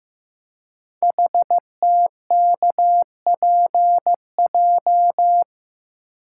Morse code, heard as a keyed tone, about 15 wpm, 700 Hz. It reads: HTKPJ
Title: Code Groups - 5 Letter:Numbers - 15wpm